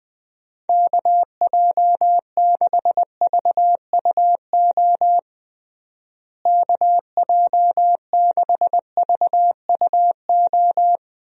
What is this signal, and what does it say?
Morse code, heard as a keyed tone, about 21 wpm, 700 Hz. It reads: KJ6VUO KJ6VUO
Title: USA Call Signs - 6 Characters - Repeated 3x - 20wpm